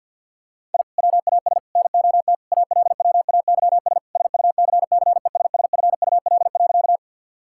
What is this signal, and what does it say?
Morse code, heard as a keyed tone, about 50 wpm, 700 Hz. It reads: I WAS NOT ALWAYS SUCCESSFUL.
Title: Sentences from Top 1000 Words - Review Part 6 - 50wpm